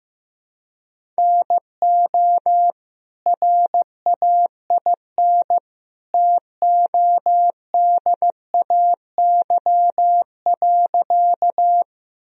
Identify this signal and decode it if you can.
Morse code, heard as a keyed tone, about 15 wpm, 700 Hz. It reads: NO RAIN TODAY.
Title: Sentences from Top 1000 Words - Review Part 1 - 15wpm